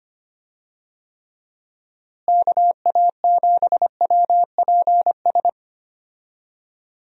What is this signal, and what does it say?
Morse code, heard as a keyed tone, about 25 wpm, 700 Hz. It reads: KA7WPS
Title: USA Call Signs - 6 Characters - Repeated 2x - 25wpm